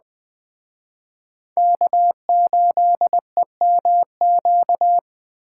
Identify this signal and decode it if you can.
Morse code, heard as a keyed tone, about 20 wpm, 700 Hz. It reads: K8EMQ
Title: Speed Racing - USA 1x3 Call Signs 17wpm